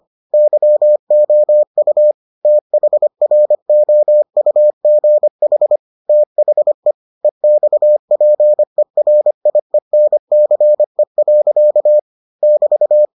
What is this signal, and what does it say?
Morse code, heard as a keyed tone, about 25 wpm, 600 Hz. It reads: YOU THROUGH THE EXPERIENCE. <BT>